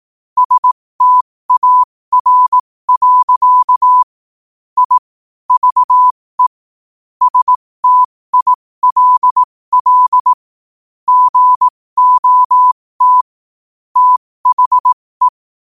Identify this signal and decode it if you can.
Morse code, heard as a keyed tone, about 18 wpm, spaced as for 15 wpm, 1000 Hz. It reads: STAR. I VE STILL GOT THE
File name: qotdfiles / 15wpm